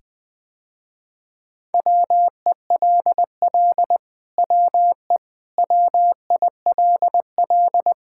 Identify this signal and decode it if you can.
Morse code, heard as a keyed tone, about 20 wpm, 700 Hz. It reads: WELL WE WILL
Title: Sentences with 7-Word Phrases - Spoken Beginning - 20wpm